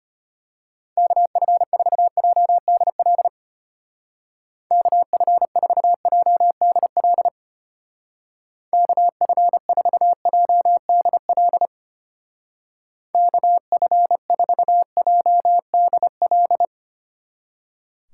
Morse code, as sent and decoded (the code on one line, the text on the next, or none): -.- ..-. ....- .--- -.. .-.. / -.- ..-. ....- .--- -.. .-.. / -.- ..-. ....- .--- -.. .-.. / -.- ..-. ....- .--- -.. .-..
KF4JDL KF4JDL KF4JDL KF4JDL